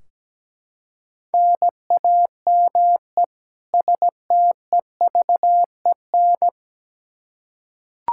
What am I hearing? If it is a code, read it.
Morse code, heard as a keyed tone, about 17 wpm, 700 Hz. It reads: NAME STEVEN
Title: Practice QSO - Exchange 1 - Beginning - 17wpm